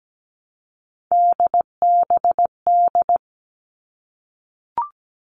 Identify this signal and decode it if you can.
Morse code, heard as a keyed tone, about 17 wpm, 700 Hz. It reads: DBD
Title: All Cut Numbers - 3-Digit 17wpm